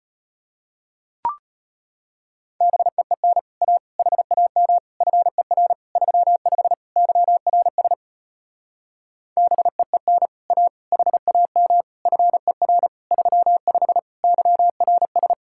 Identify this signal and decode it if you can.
Morse code, heard as a keyed tone, about 33 wpm, 700 Hz. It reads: BEEN A HAM FER 35 YRS BEEN A HAM FER 35 YRS